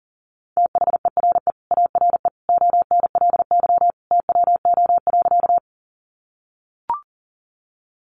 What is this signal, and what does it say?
Morse code, heard as a keyed tone, about 40 wpm, 700 Hz. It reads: THERE ARE ONLY TWO.
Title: Sentences from Top 200 Words 40wpm